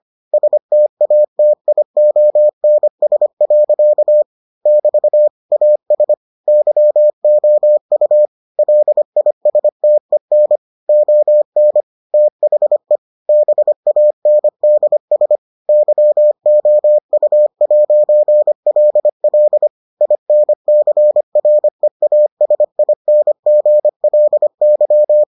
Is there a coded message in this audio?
Yes: Morse code from 0.3 to 25.2 s, heard as a keyed tone, about 25 wpm, 600 Hz. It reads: STATIONS. <BT> AS YOU LISTEN ON THE BANDS YOU'LL INCREASINGLY